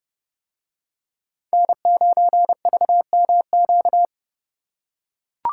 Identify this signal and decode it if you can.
Morse code, heard as a keyed tone, about 30 wpm, 700 Hz. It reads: N9VMQ